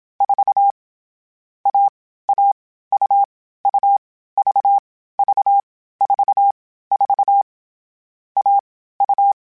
Morse code, heard as a keyed tone, about 27 wpm, 800 Hz. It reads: 4 A A U U V V 4 4 A U
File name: Difficult Combos / Difficult Combos-mp3s / AUV AUV4 @25